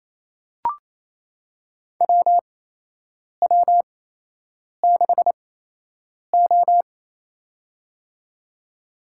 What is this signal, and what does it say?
Morse code, heard as a keyed tone, about 28 wpm, spaced as for 8 wpm, 700 Hz. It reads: WW6O
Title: USA Call Signs - 4 Characters - 8x Character Spacing - 28wpm